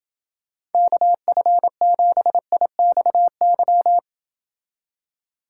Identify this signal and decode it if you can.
Morse code, heard as a keyed tone, about 28 wpm, 700 Hz. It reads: KF7IXY